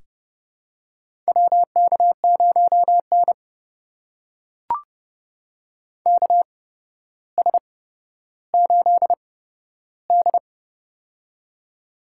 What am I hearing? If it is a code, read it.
Morse code, heard as a keyed tone, about 30 wpm, spaced as for 13 wpm, 700 Hz. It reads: WK0N KS8D
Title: USA Call Signs - 4 Characters - 8x Character Spacing - 30wpm